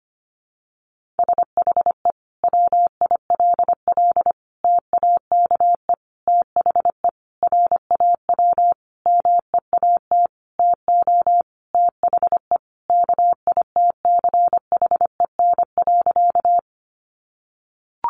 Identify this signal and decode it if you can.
Morse code, heard as a keyed tone, about 25 wpm, 700 Hz. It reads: SHE WILL TAKE THE RAW MEAT TO THE KITCHEN.
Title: Sentences from Top 1500 Words 25wpm